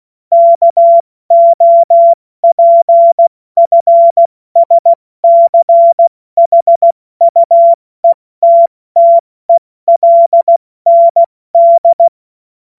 Morse code, heard as a keyed tone, about 16 wpm, 680 Hz. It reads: KOPFSCHUETTELND
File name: Morsekurs / Witze / Witz 02. August 2022 70 F80